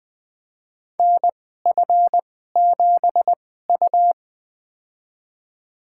Morse code, heard as a keyed tone, about 20 wpm, spaced as for 14 wpm, 700 Hz. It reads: NF7U
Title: USA Call Signs - 4 Characters - 2x Character Spacing - 20wpm